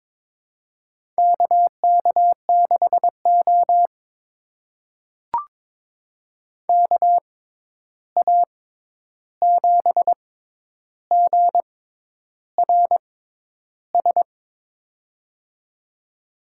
Morse code, heard as a keyed tone, about 21 wpm, spaced as for 10 wpm, 700 Hz. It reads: KK6O KA7GRS
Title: USA Call Signs - 6x Character Spacing - 22wpm